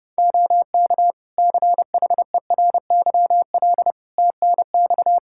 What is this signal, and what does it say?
Morse code, heard as a keyed tone, about 30 wpm, 700 Hz. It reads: OK CHERYL TNX
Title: Practice QSO - Exchange 4 - Ending - 30wpm